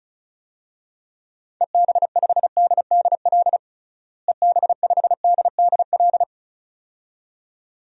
Morse code, heard as a keyed tone, about 34 wpm, 700 Hz. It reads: EB5DDL EB5DDL
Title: DXCC-100 Call Signs - 6 Characters - Repeated 2x - 35wpm